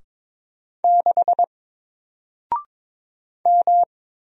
Morse code, heard as a keyed tone, about 22 wpm, 700 Hz. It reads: 6 M
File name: Single-Letter-Number-22wpm